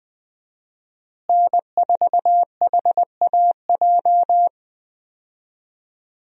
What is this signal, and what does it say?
Morse code, heard as a keyed tone, about 21 wpm, 700 Hz. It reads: N4HAJ
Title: USA Call Signs - 5 Characters - Repeated 3x - 20wpm